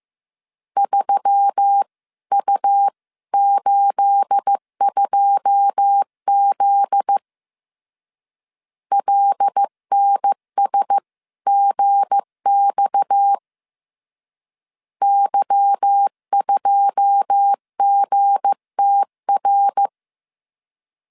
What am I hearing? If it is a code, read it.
Morse code, heard as a keyed tone, about 15 wpm, 800 Hz. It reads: 3U82Z LNSGX Y2GTR
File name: CW / DET / 1 9 1500C S C 14wpm 3245